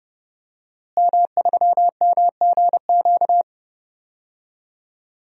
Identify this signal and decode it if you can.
Morse code, heard as a keyed tone, about 32 wpm, 700 Hz. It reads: M3MGQ
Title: DXCC-100 Call Signs - 5 Characters - Repeated 3x - 30wpm